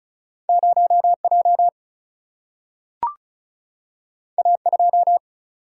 Morse code, heard as a keyed tone, about 36 wpm, 700 Hz. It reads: 0J A2
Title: Single-Letter Single-Number 35wpm